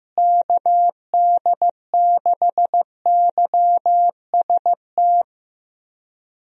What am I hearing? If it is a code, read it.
Morse code, heard as a keyed tone, about 15 wpm, 700 Hz. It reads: KD6YST
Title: USA Call Signs - 6 Characters - Repeated 3x - 15wpm